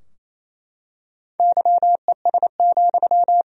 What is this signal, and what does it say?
Morse code, heard as a keyed tone, about 27 wpm, 700 Hz. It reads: YES,
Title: Sentences from Top 1000 Words - Review Part 3 - 28wpm